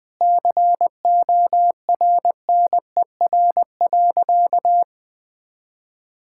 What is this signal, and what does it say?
Morse code, heard as a keyed tone, about 20 wpm, 700 Hz. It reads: CORNER.
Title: Sentences from Top 1600 Words 20wpm